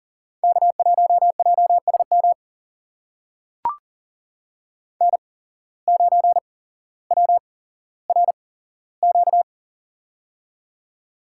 Morse code, heard as a keyed tone, about 40 wpm, spaced as for 17 wpm, 700 Hz. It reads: K1JSM N9WRQ
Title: USA Call Signs - 5 Characters - 8x Character Spacing - 40wpm